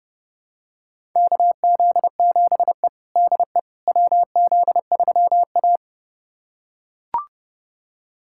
Morse code, heard as a keyed tone, about 30 wpm, 700 Hz. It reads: KZ7E DE WZ3A
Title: USA Callsign DE Callsign - 4 Characters - 30wpm